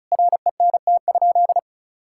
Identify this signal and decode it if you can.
Morse code, heard as a keyed tone, about 36 wpm, 700 Hz. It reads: RENT?
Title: Sentences from Top 1000 Words - Review Part 5 - 35wpm